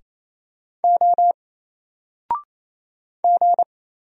Morse code, heard as a keyed tone, about 28 wpm, 700 Hz. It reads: O G
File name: Single-Letter-Number-28wpm